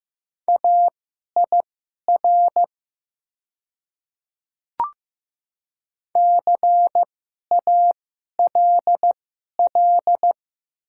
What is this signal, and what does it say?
Morse code, heard as a keyed tone, about 15 wpm, spaced as for 11 wpm, 700 Hz. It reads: AIR CALL
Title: 2x Character Spacing - Top 500 Words - 15wpm